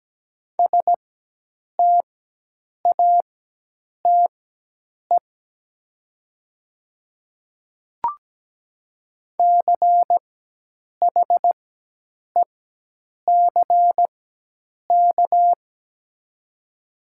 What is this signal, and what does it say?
Morse code, heard as a keyed tone, about 17 wpm, spaced as for 8 wpm, 700 Hz. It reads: STATE CHECK
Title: 4x Character Spacing - Top 500 Words - 17wpm